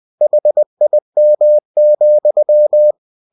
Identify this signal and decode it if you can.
Morse code, heard as a keyed tone, about 20 wpm, 600 Hz. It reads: HIM,
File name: cw-books / winnie-the-pooh / winnie-20wpm-page-0043